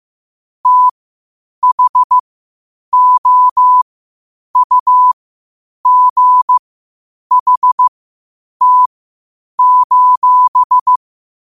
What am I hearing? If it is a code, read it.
Morse code, heard as a keyed tone, about 15 wpm, spaced as for 8 wpm, 1000 Hz. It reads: THOUGHT:
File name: qotdfiles / 8wpm